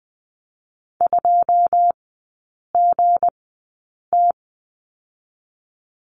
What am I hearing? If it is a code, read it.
Morse code, heard as a keyed tone, about 20 wpm, spaced as for 8 wpm, 700 Hz. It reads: 2GT
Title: 205 Lesson 33e - Letter j - Three Characters 20wpm